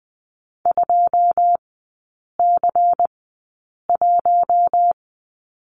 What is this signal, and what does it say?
Morse code, heard as a keyed tone, about 20 wpm, spaced as for 8 wpm, 700 Hz. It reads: 2C1
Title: 093 Lesson 16e - Letter c - Three Characters 20wpm